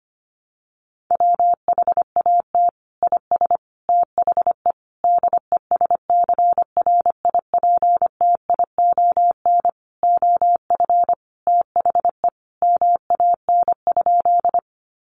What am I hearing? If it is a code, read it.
Morse code, heard as a keyed tone, about 25 wpm, 700 Hz. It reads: WHAT IS THE DESCRIPTION OF THE MAN?
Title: Sentences from Top 1300 Words 25wpm